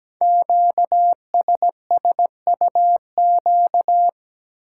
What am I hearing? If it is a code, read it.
Morse code, heard as a keyed tone, about 17 wpm, 700 Hz. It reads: QSSUQ
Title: Code Groups - 5 Letters - 17wpm